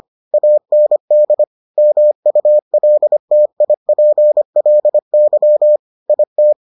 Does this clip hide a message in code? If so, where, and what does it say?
Yes: Morse code from 0.3 to 6.5 s, heard as a keyed tone, about 25 wpm, 600 Hz. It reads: AND MULTIPLY IT